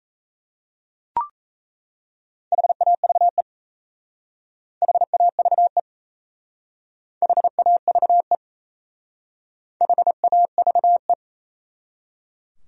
Morse code, heard as a keyed tone, about 34 wpm, 700 Hz. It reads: HAVE HAVE HAVE HAVE
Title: Speed-Racing Top 100 Words 28wpm